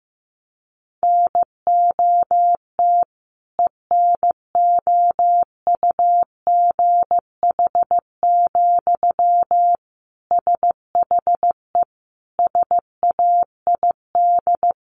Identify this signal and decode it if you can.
Morse code, heard as a keyed tone, about 14 wpm, 700 Hz. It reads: NOT ENOUGH, SHE SAID
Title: Sentences from Top 1500 Words 15wpm